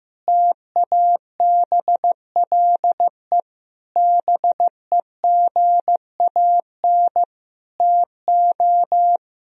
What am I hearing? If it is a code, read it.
Morse code, heard as a keyed tone, about 15 wpm, 700 Hz. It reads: TABLE BEGAN TO
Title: Sentences from Top 1000 Words - Review Part 2 - 15wpm